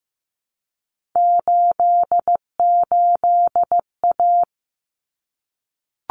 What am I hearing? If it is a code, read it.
Morse code, heard as a keyed tone, about 15 wpm, 700 Hz. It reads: 88A
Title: Common Cut Numbers - 3-Digit 15wpm